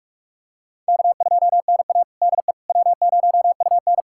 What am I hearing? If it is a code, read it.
Morse code, heard as a keyed tone, about 46 wpm, 700 Hz. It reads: K2NA DE W0UN
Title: USA Callsign DE Callsign - 4 Characters - 45wpm